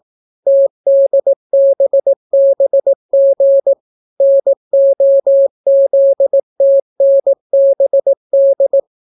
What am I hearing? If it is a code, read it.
Morse code, heard as a keyed tone, about 18 wpm, 550 Hz. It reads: TDBBG NOZTNBD